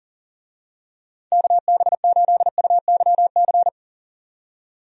Morse code, heard as a keyed tone, about 40 wpm, 700 Hz. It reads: KB8UYC
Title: USA Call Signs - 6 Characters - 40wpm